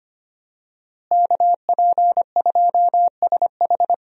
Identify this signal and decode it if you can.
Morse code, heard as a keyed tone, about 25 wpm, 700 Hz. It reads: KP2SH
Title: DXCC-100 Call Signs - 5 Characters - Repeated 2x - 25wpm